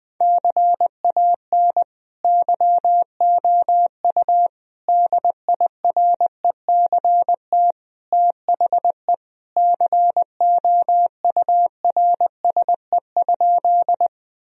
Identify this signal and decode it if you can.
Morse code, heard as a keyed tone, about 20 wpm, 700 Hz. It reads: CAN YOU DIRECT THE COURSE?
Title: Sentences from Top 1000 Words - Review Part 2 - 20wpm